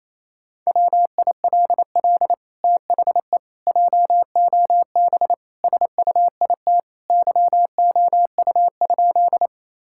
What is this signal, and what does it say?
Morse code, heard as a keyed tone, about 28 wpm, 700 Hz. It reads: WILL THE JOB SUIT YOU?
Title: Sentences from Top 1600 Words 28wpm